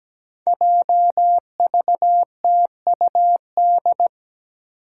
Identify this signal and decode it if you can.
Morse code, heard as a keyed tone, about 17 wpm, 700 Hz. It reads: JVTUD
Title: Five Challenging Letter-Number Combinations 17wpm